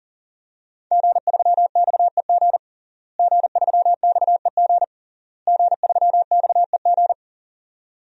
Morse code, heard as a keyed tone, about 37 wpm, 700 Hz. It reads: G3XEG G3XEG G3XEG
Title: DXCC-100 Call Signs - 5 Characters - Repeated 3x - 40wpm